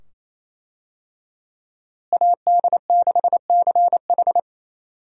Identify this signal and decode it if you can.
Morse code, heard as a keyed tone, about 27 wpm, 700 Hz. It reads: AD6CH
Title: USA Call Signs - 5 Characters - Repeated 2x - 28wpm